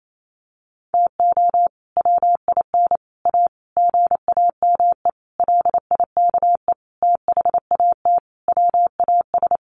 Morse code, heard as a keyed tone, about 28 wpm, 700 Hz. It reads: TO WIN A GAME LIKE THAT WAS
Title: Sentences from Top 1500 Words 28wpm